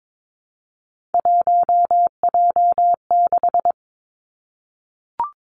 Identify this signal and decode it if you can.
Morse code, heard as a keyed tone, about 22 wpm, 700 Hz. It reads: 1J6
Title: Three Challenging Letter-Number Combinations 22wpm